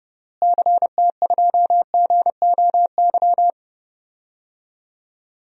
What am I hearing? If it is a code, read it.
Morse code, heard as a keyed tone, about 30 wpm, 700 Hz. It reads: CT2GOY
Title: DXCC-100 Call Signs - 6 Characters - Repeated 2x - 30wpm